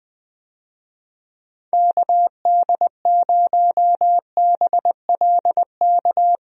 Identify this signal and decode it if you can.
Morse code, heard as a keyed tone, about 21 wpm, 700 Hz. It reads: KD0BLK
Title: USA Call Signs - 6 Characters - Repeated 3x - 20wpm